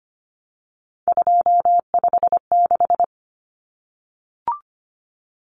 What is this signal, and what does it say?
Morse code, heard as a keyed tone, about 25 wpm, 700 Hz. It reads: 256
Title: Common Cut Numbers - 3-Digit 25wpm